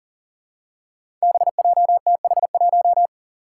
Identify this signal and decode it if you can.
Morse code, heard as a keyed tone, about 40 wpm, 700 Hz. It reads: BJTH1
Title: Five Challenging Letter-Number Combinations 40wpm